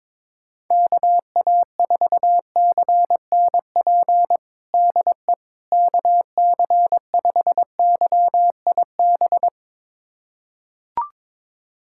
Morse code, heard as a keyed tone, about 22 wpm, 700 Hz. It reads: KA4CNP DE KC5YIB
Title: USA Callsign DE Callsign - 6 Characters - 22wpm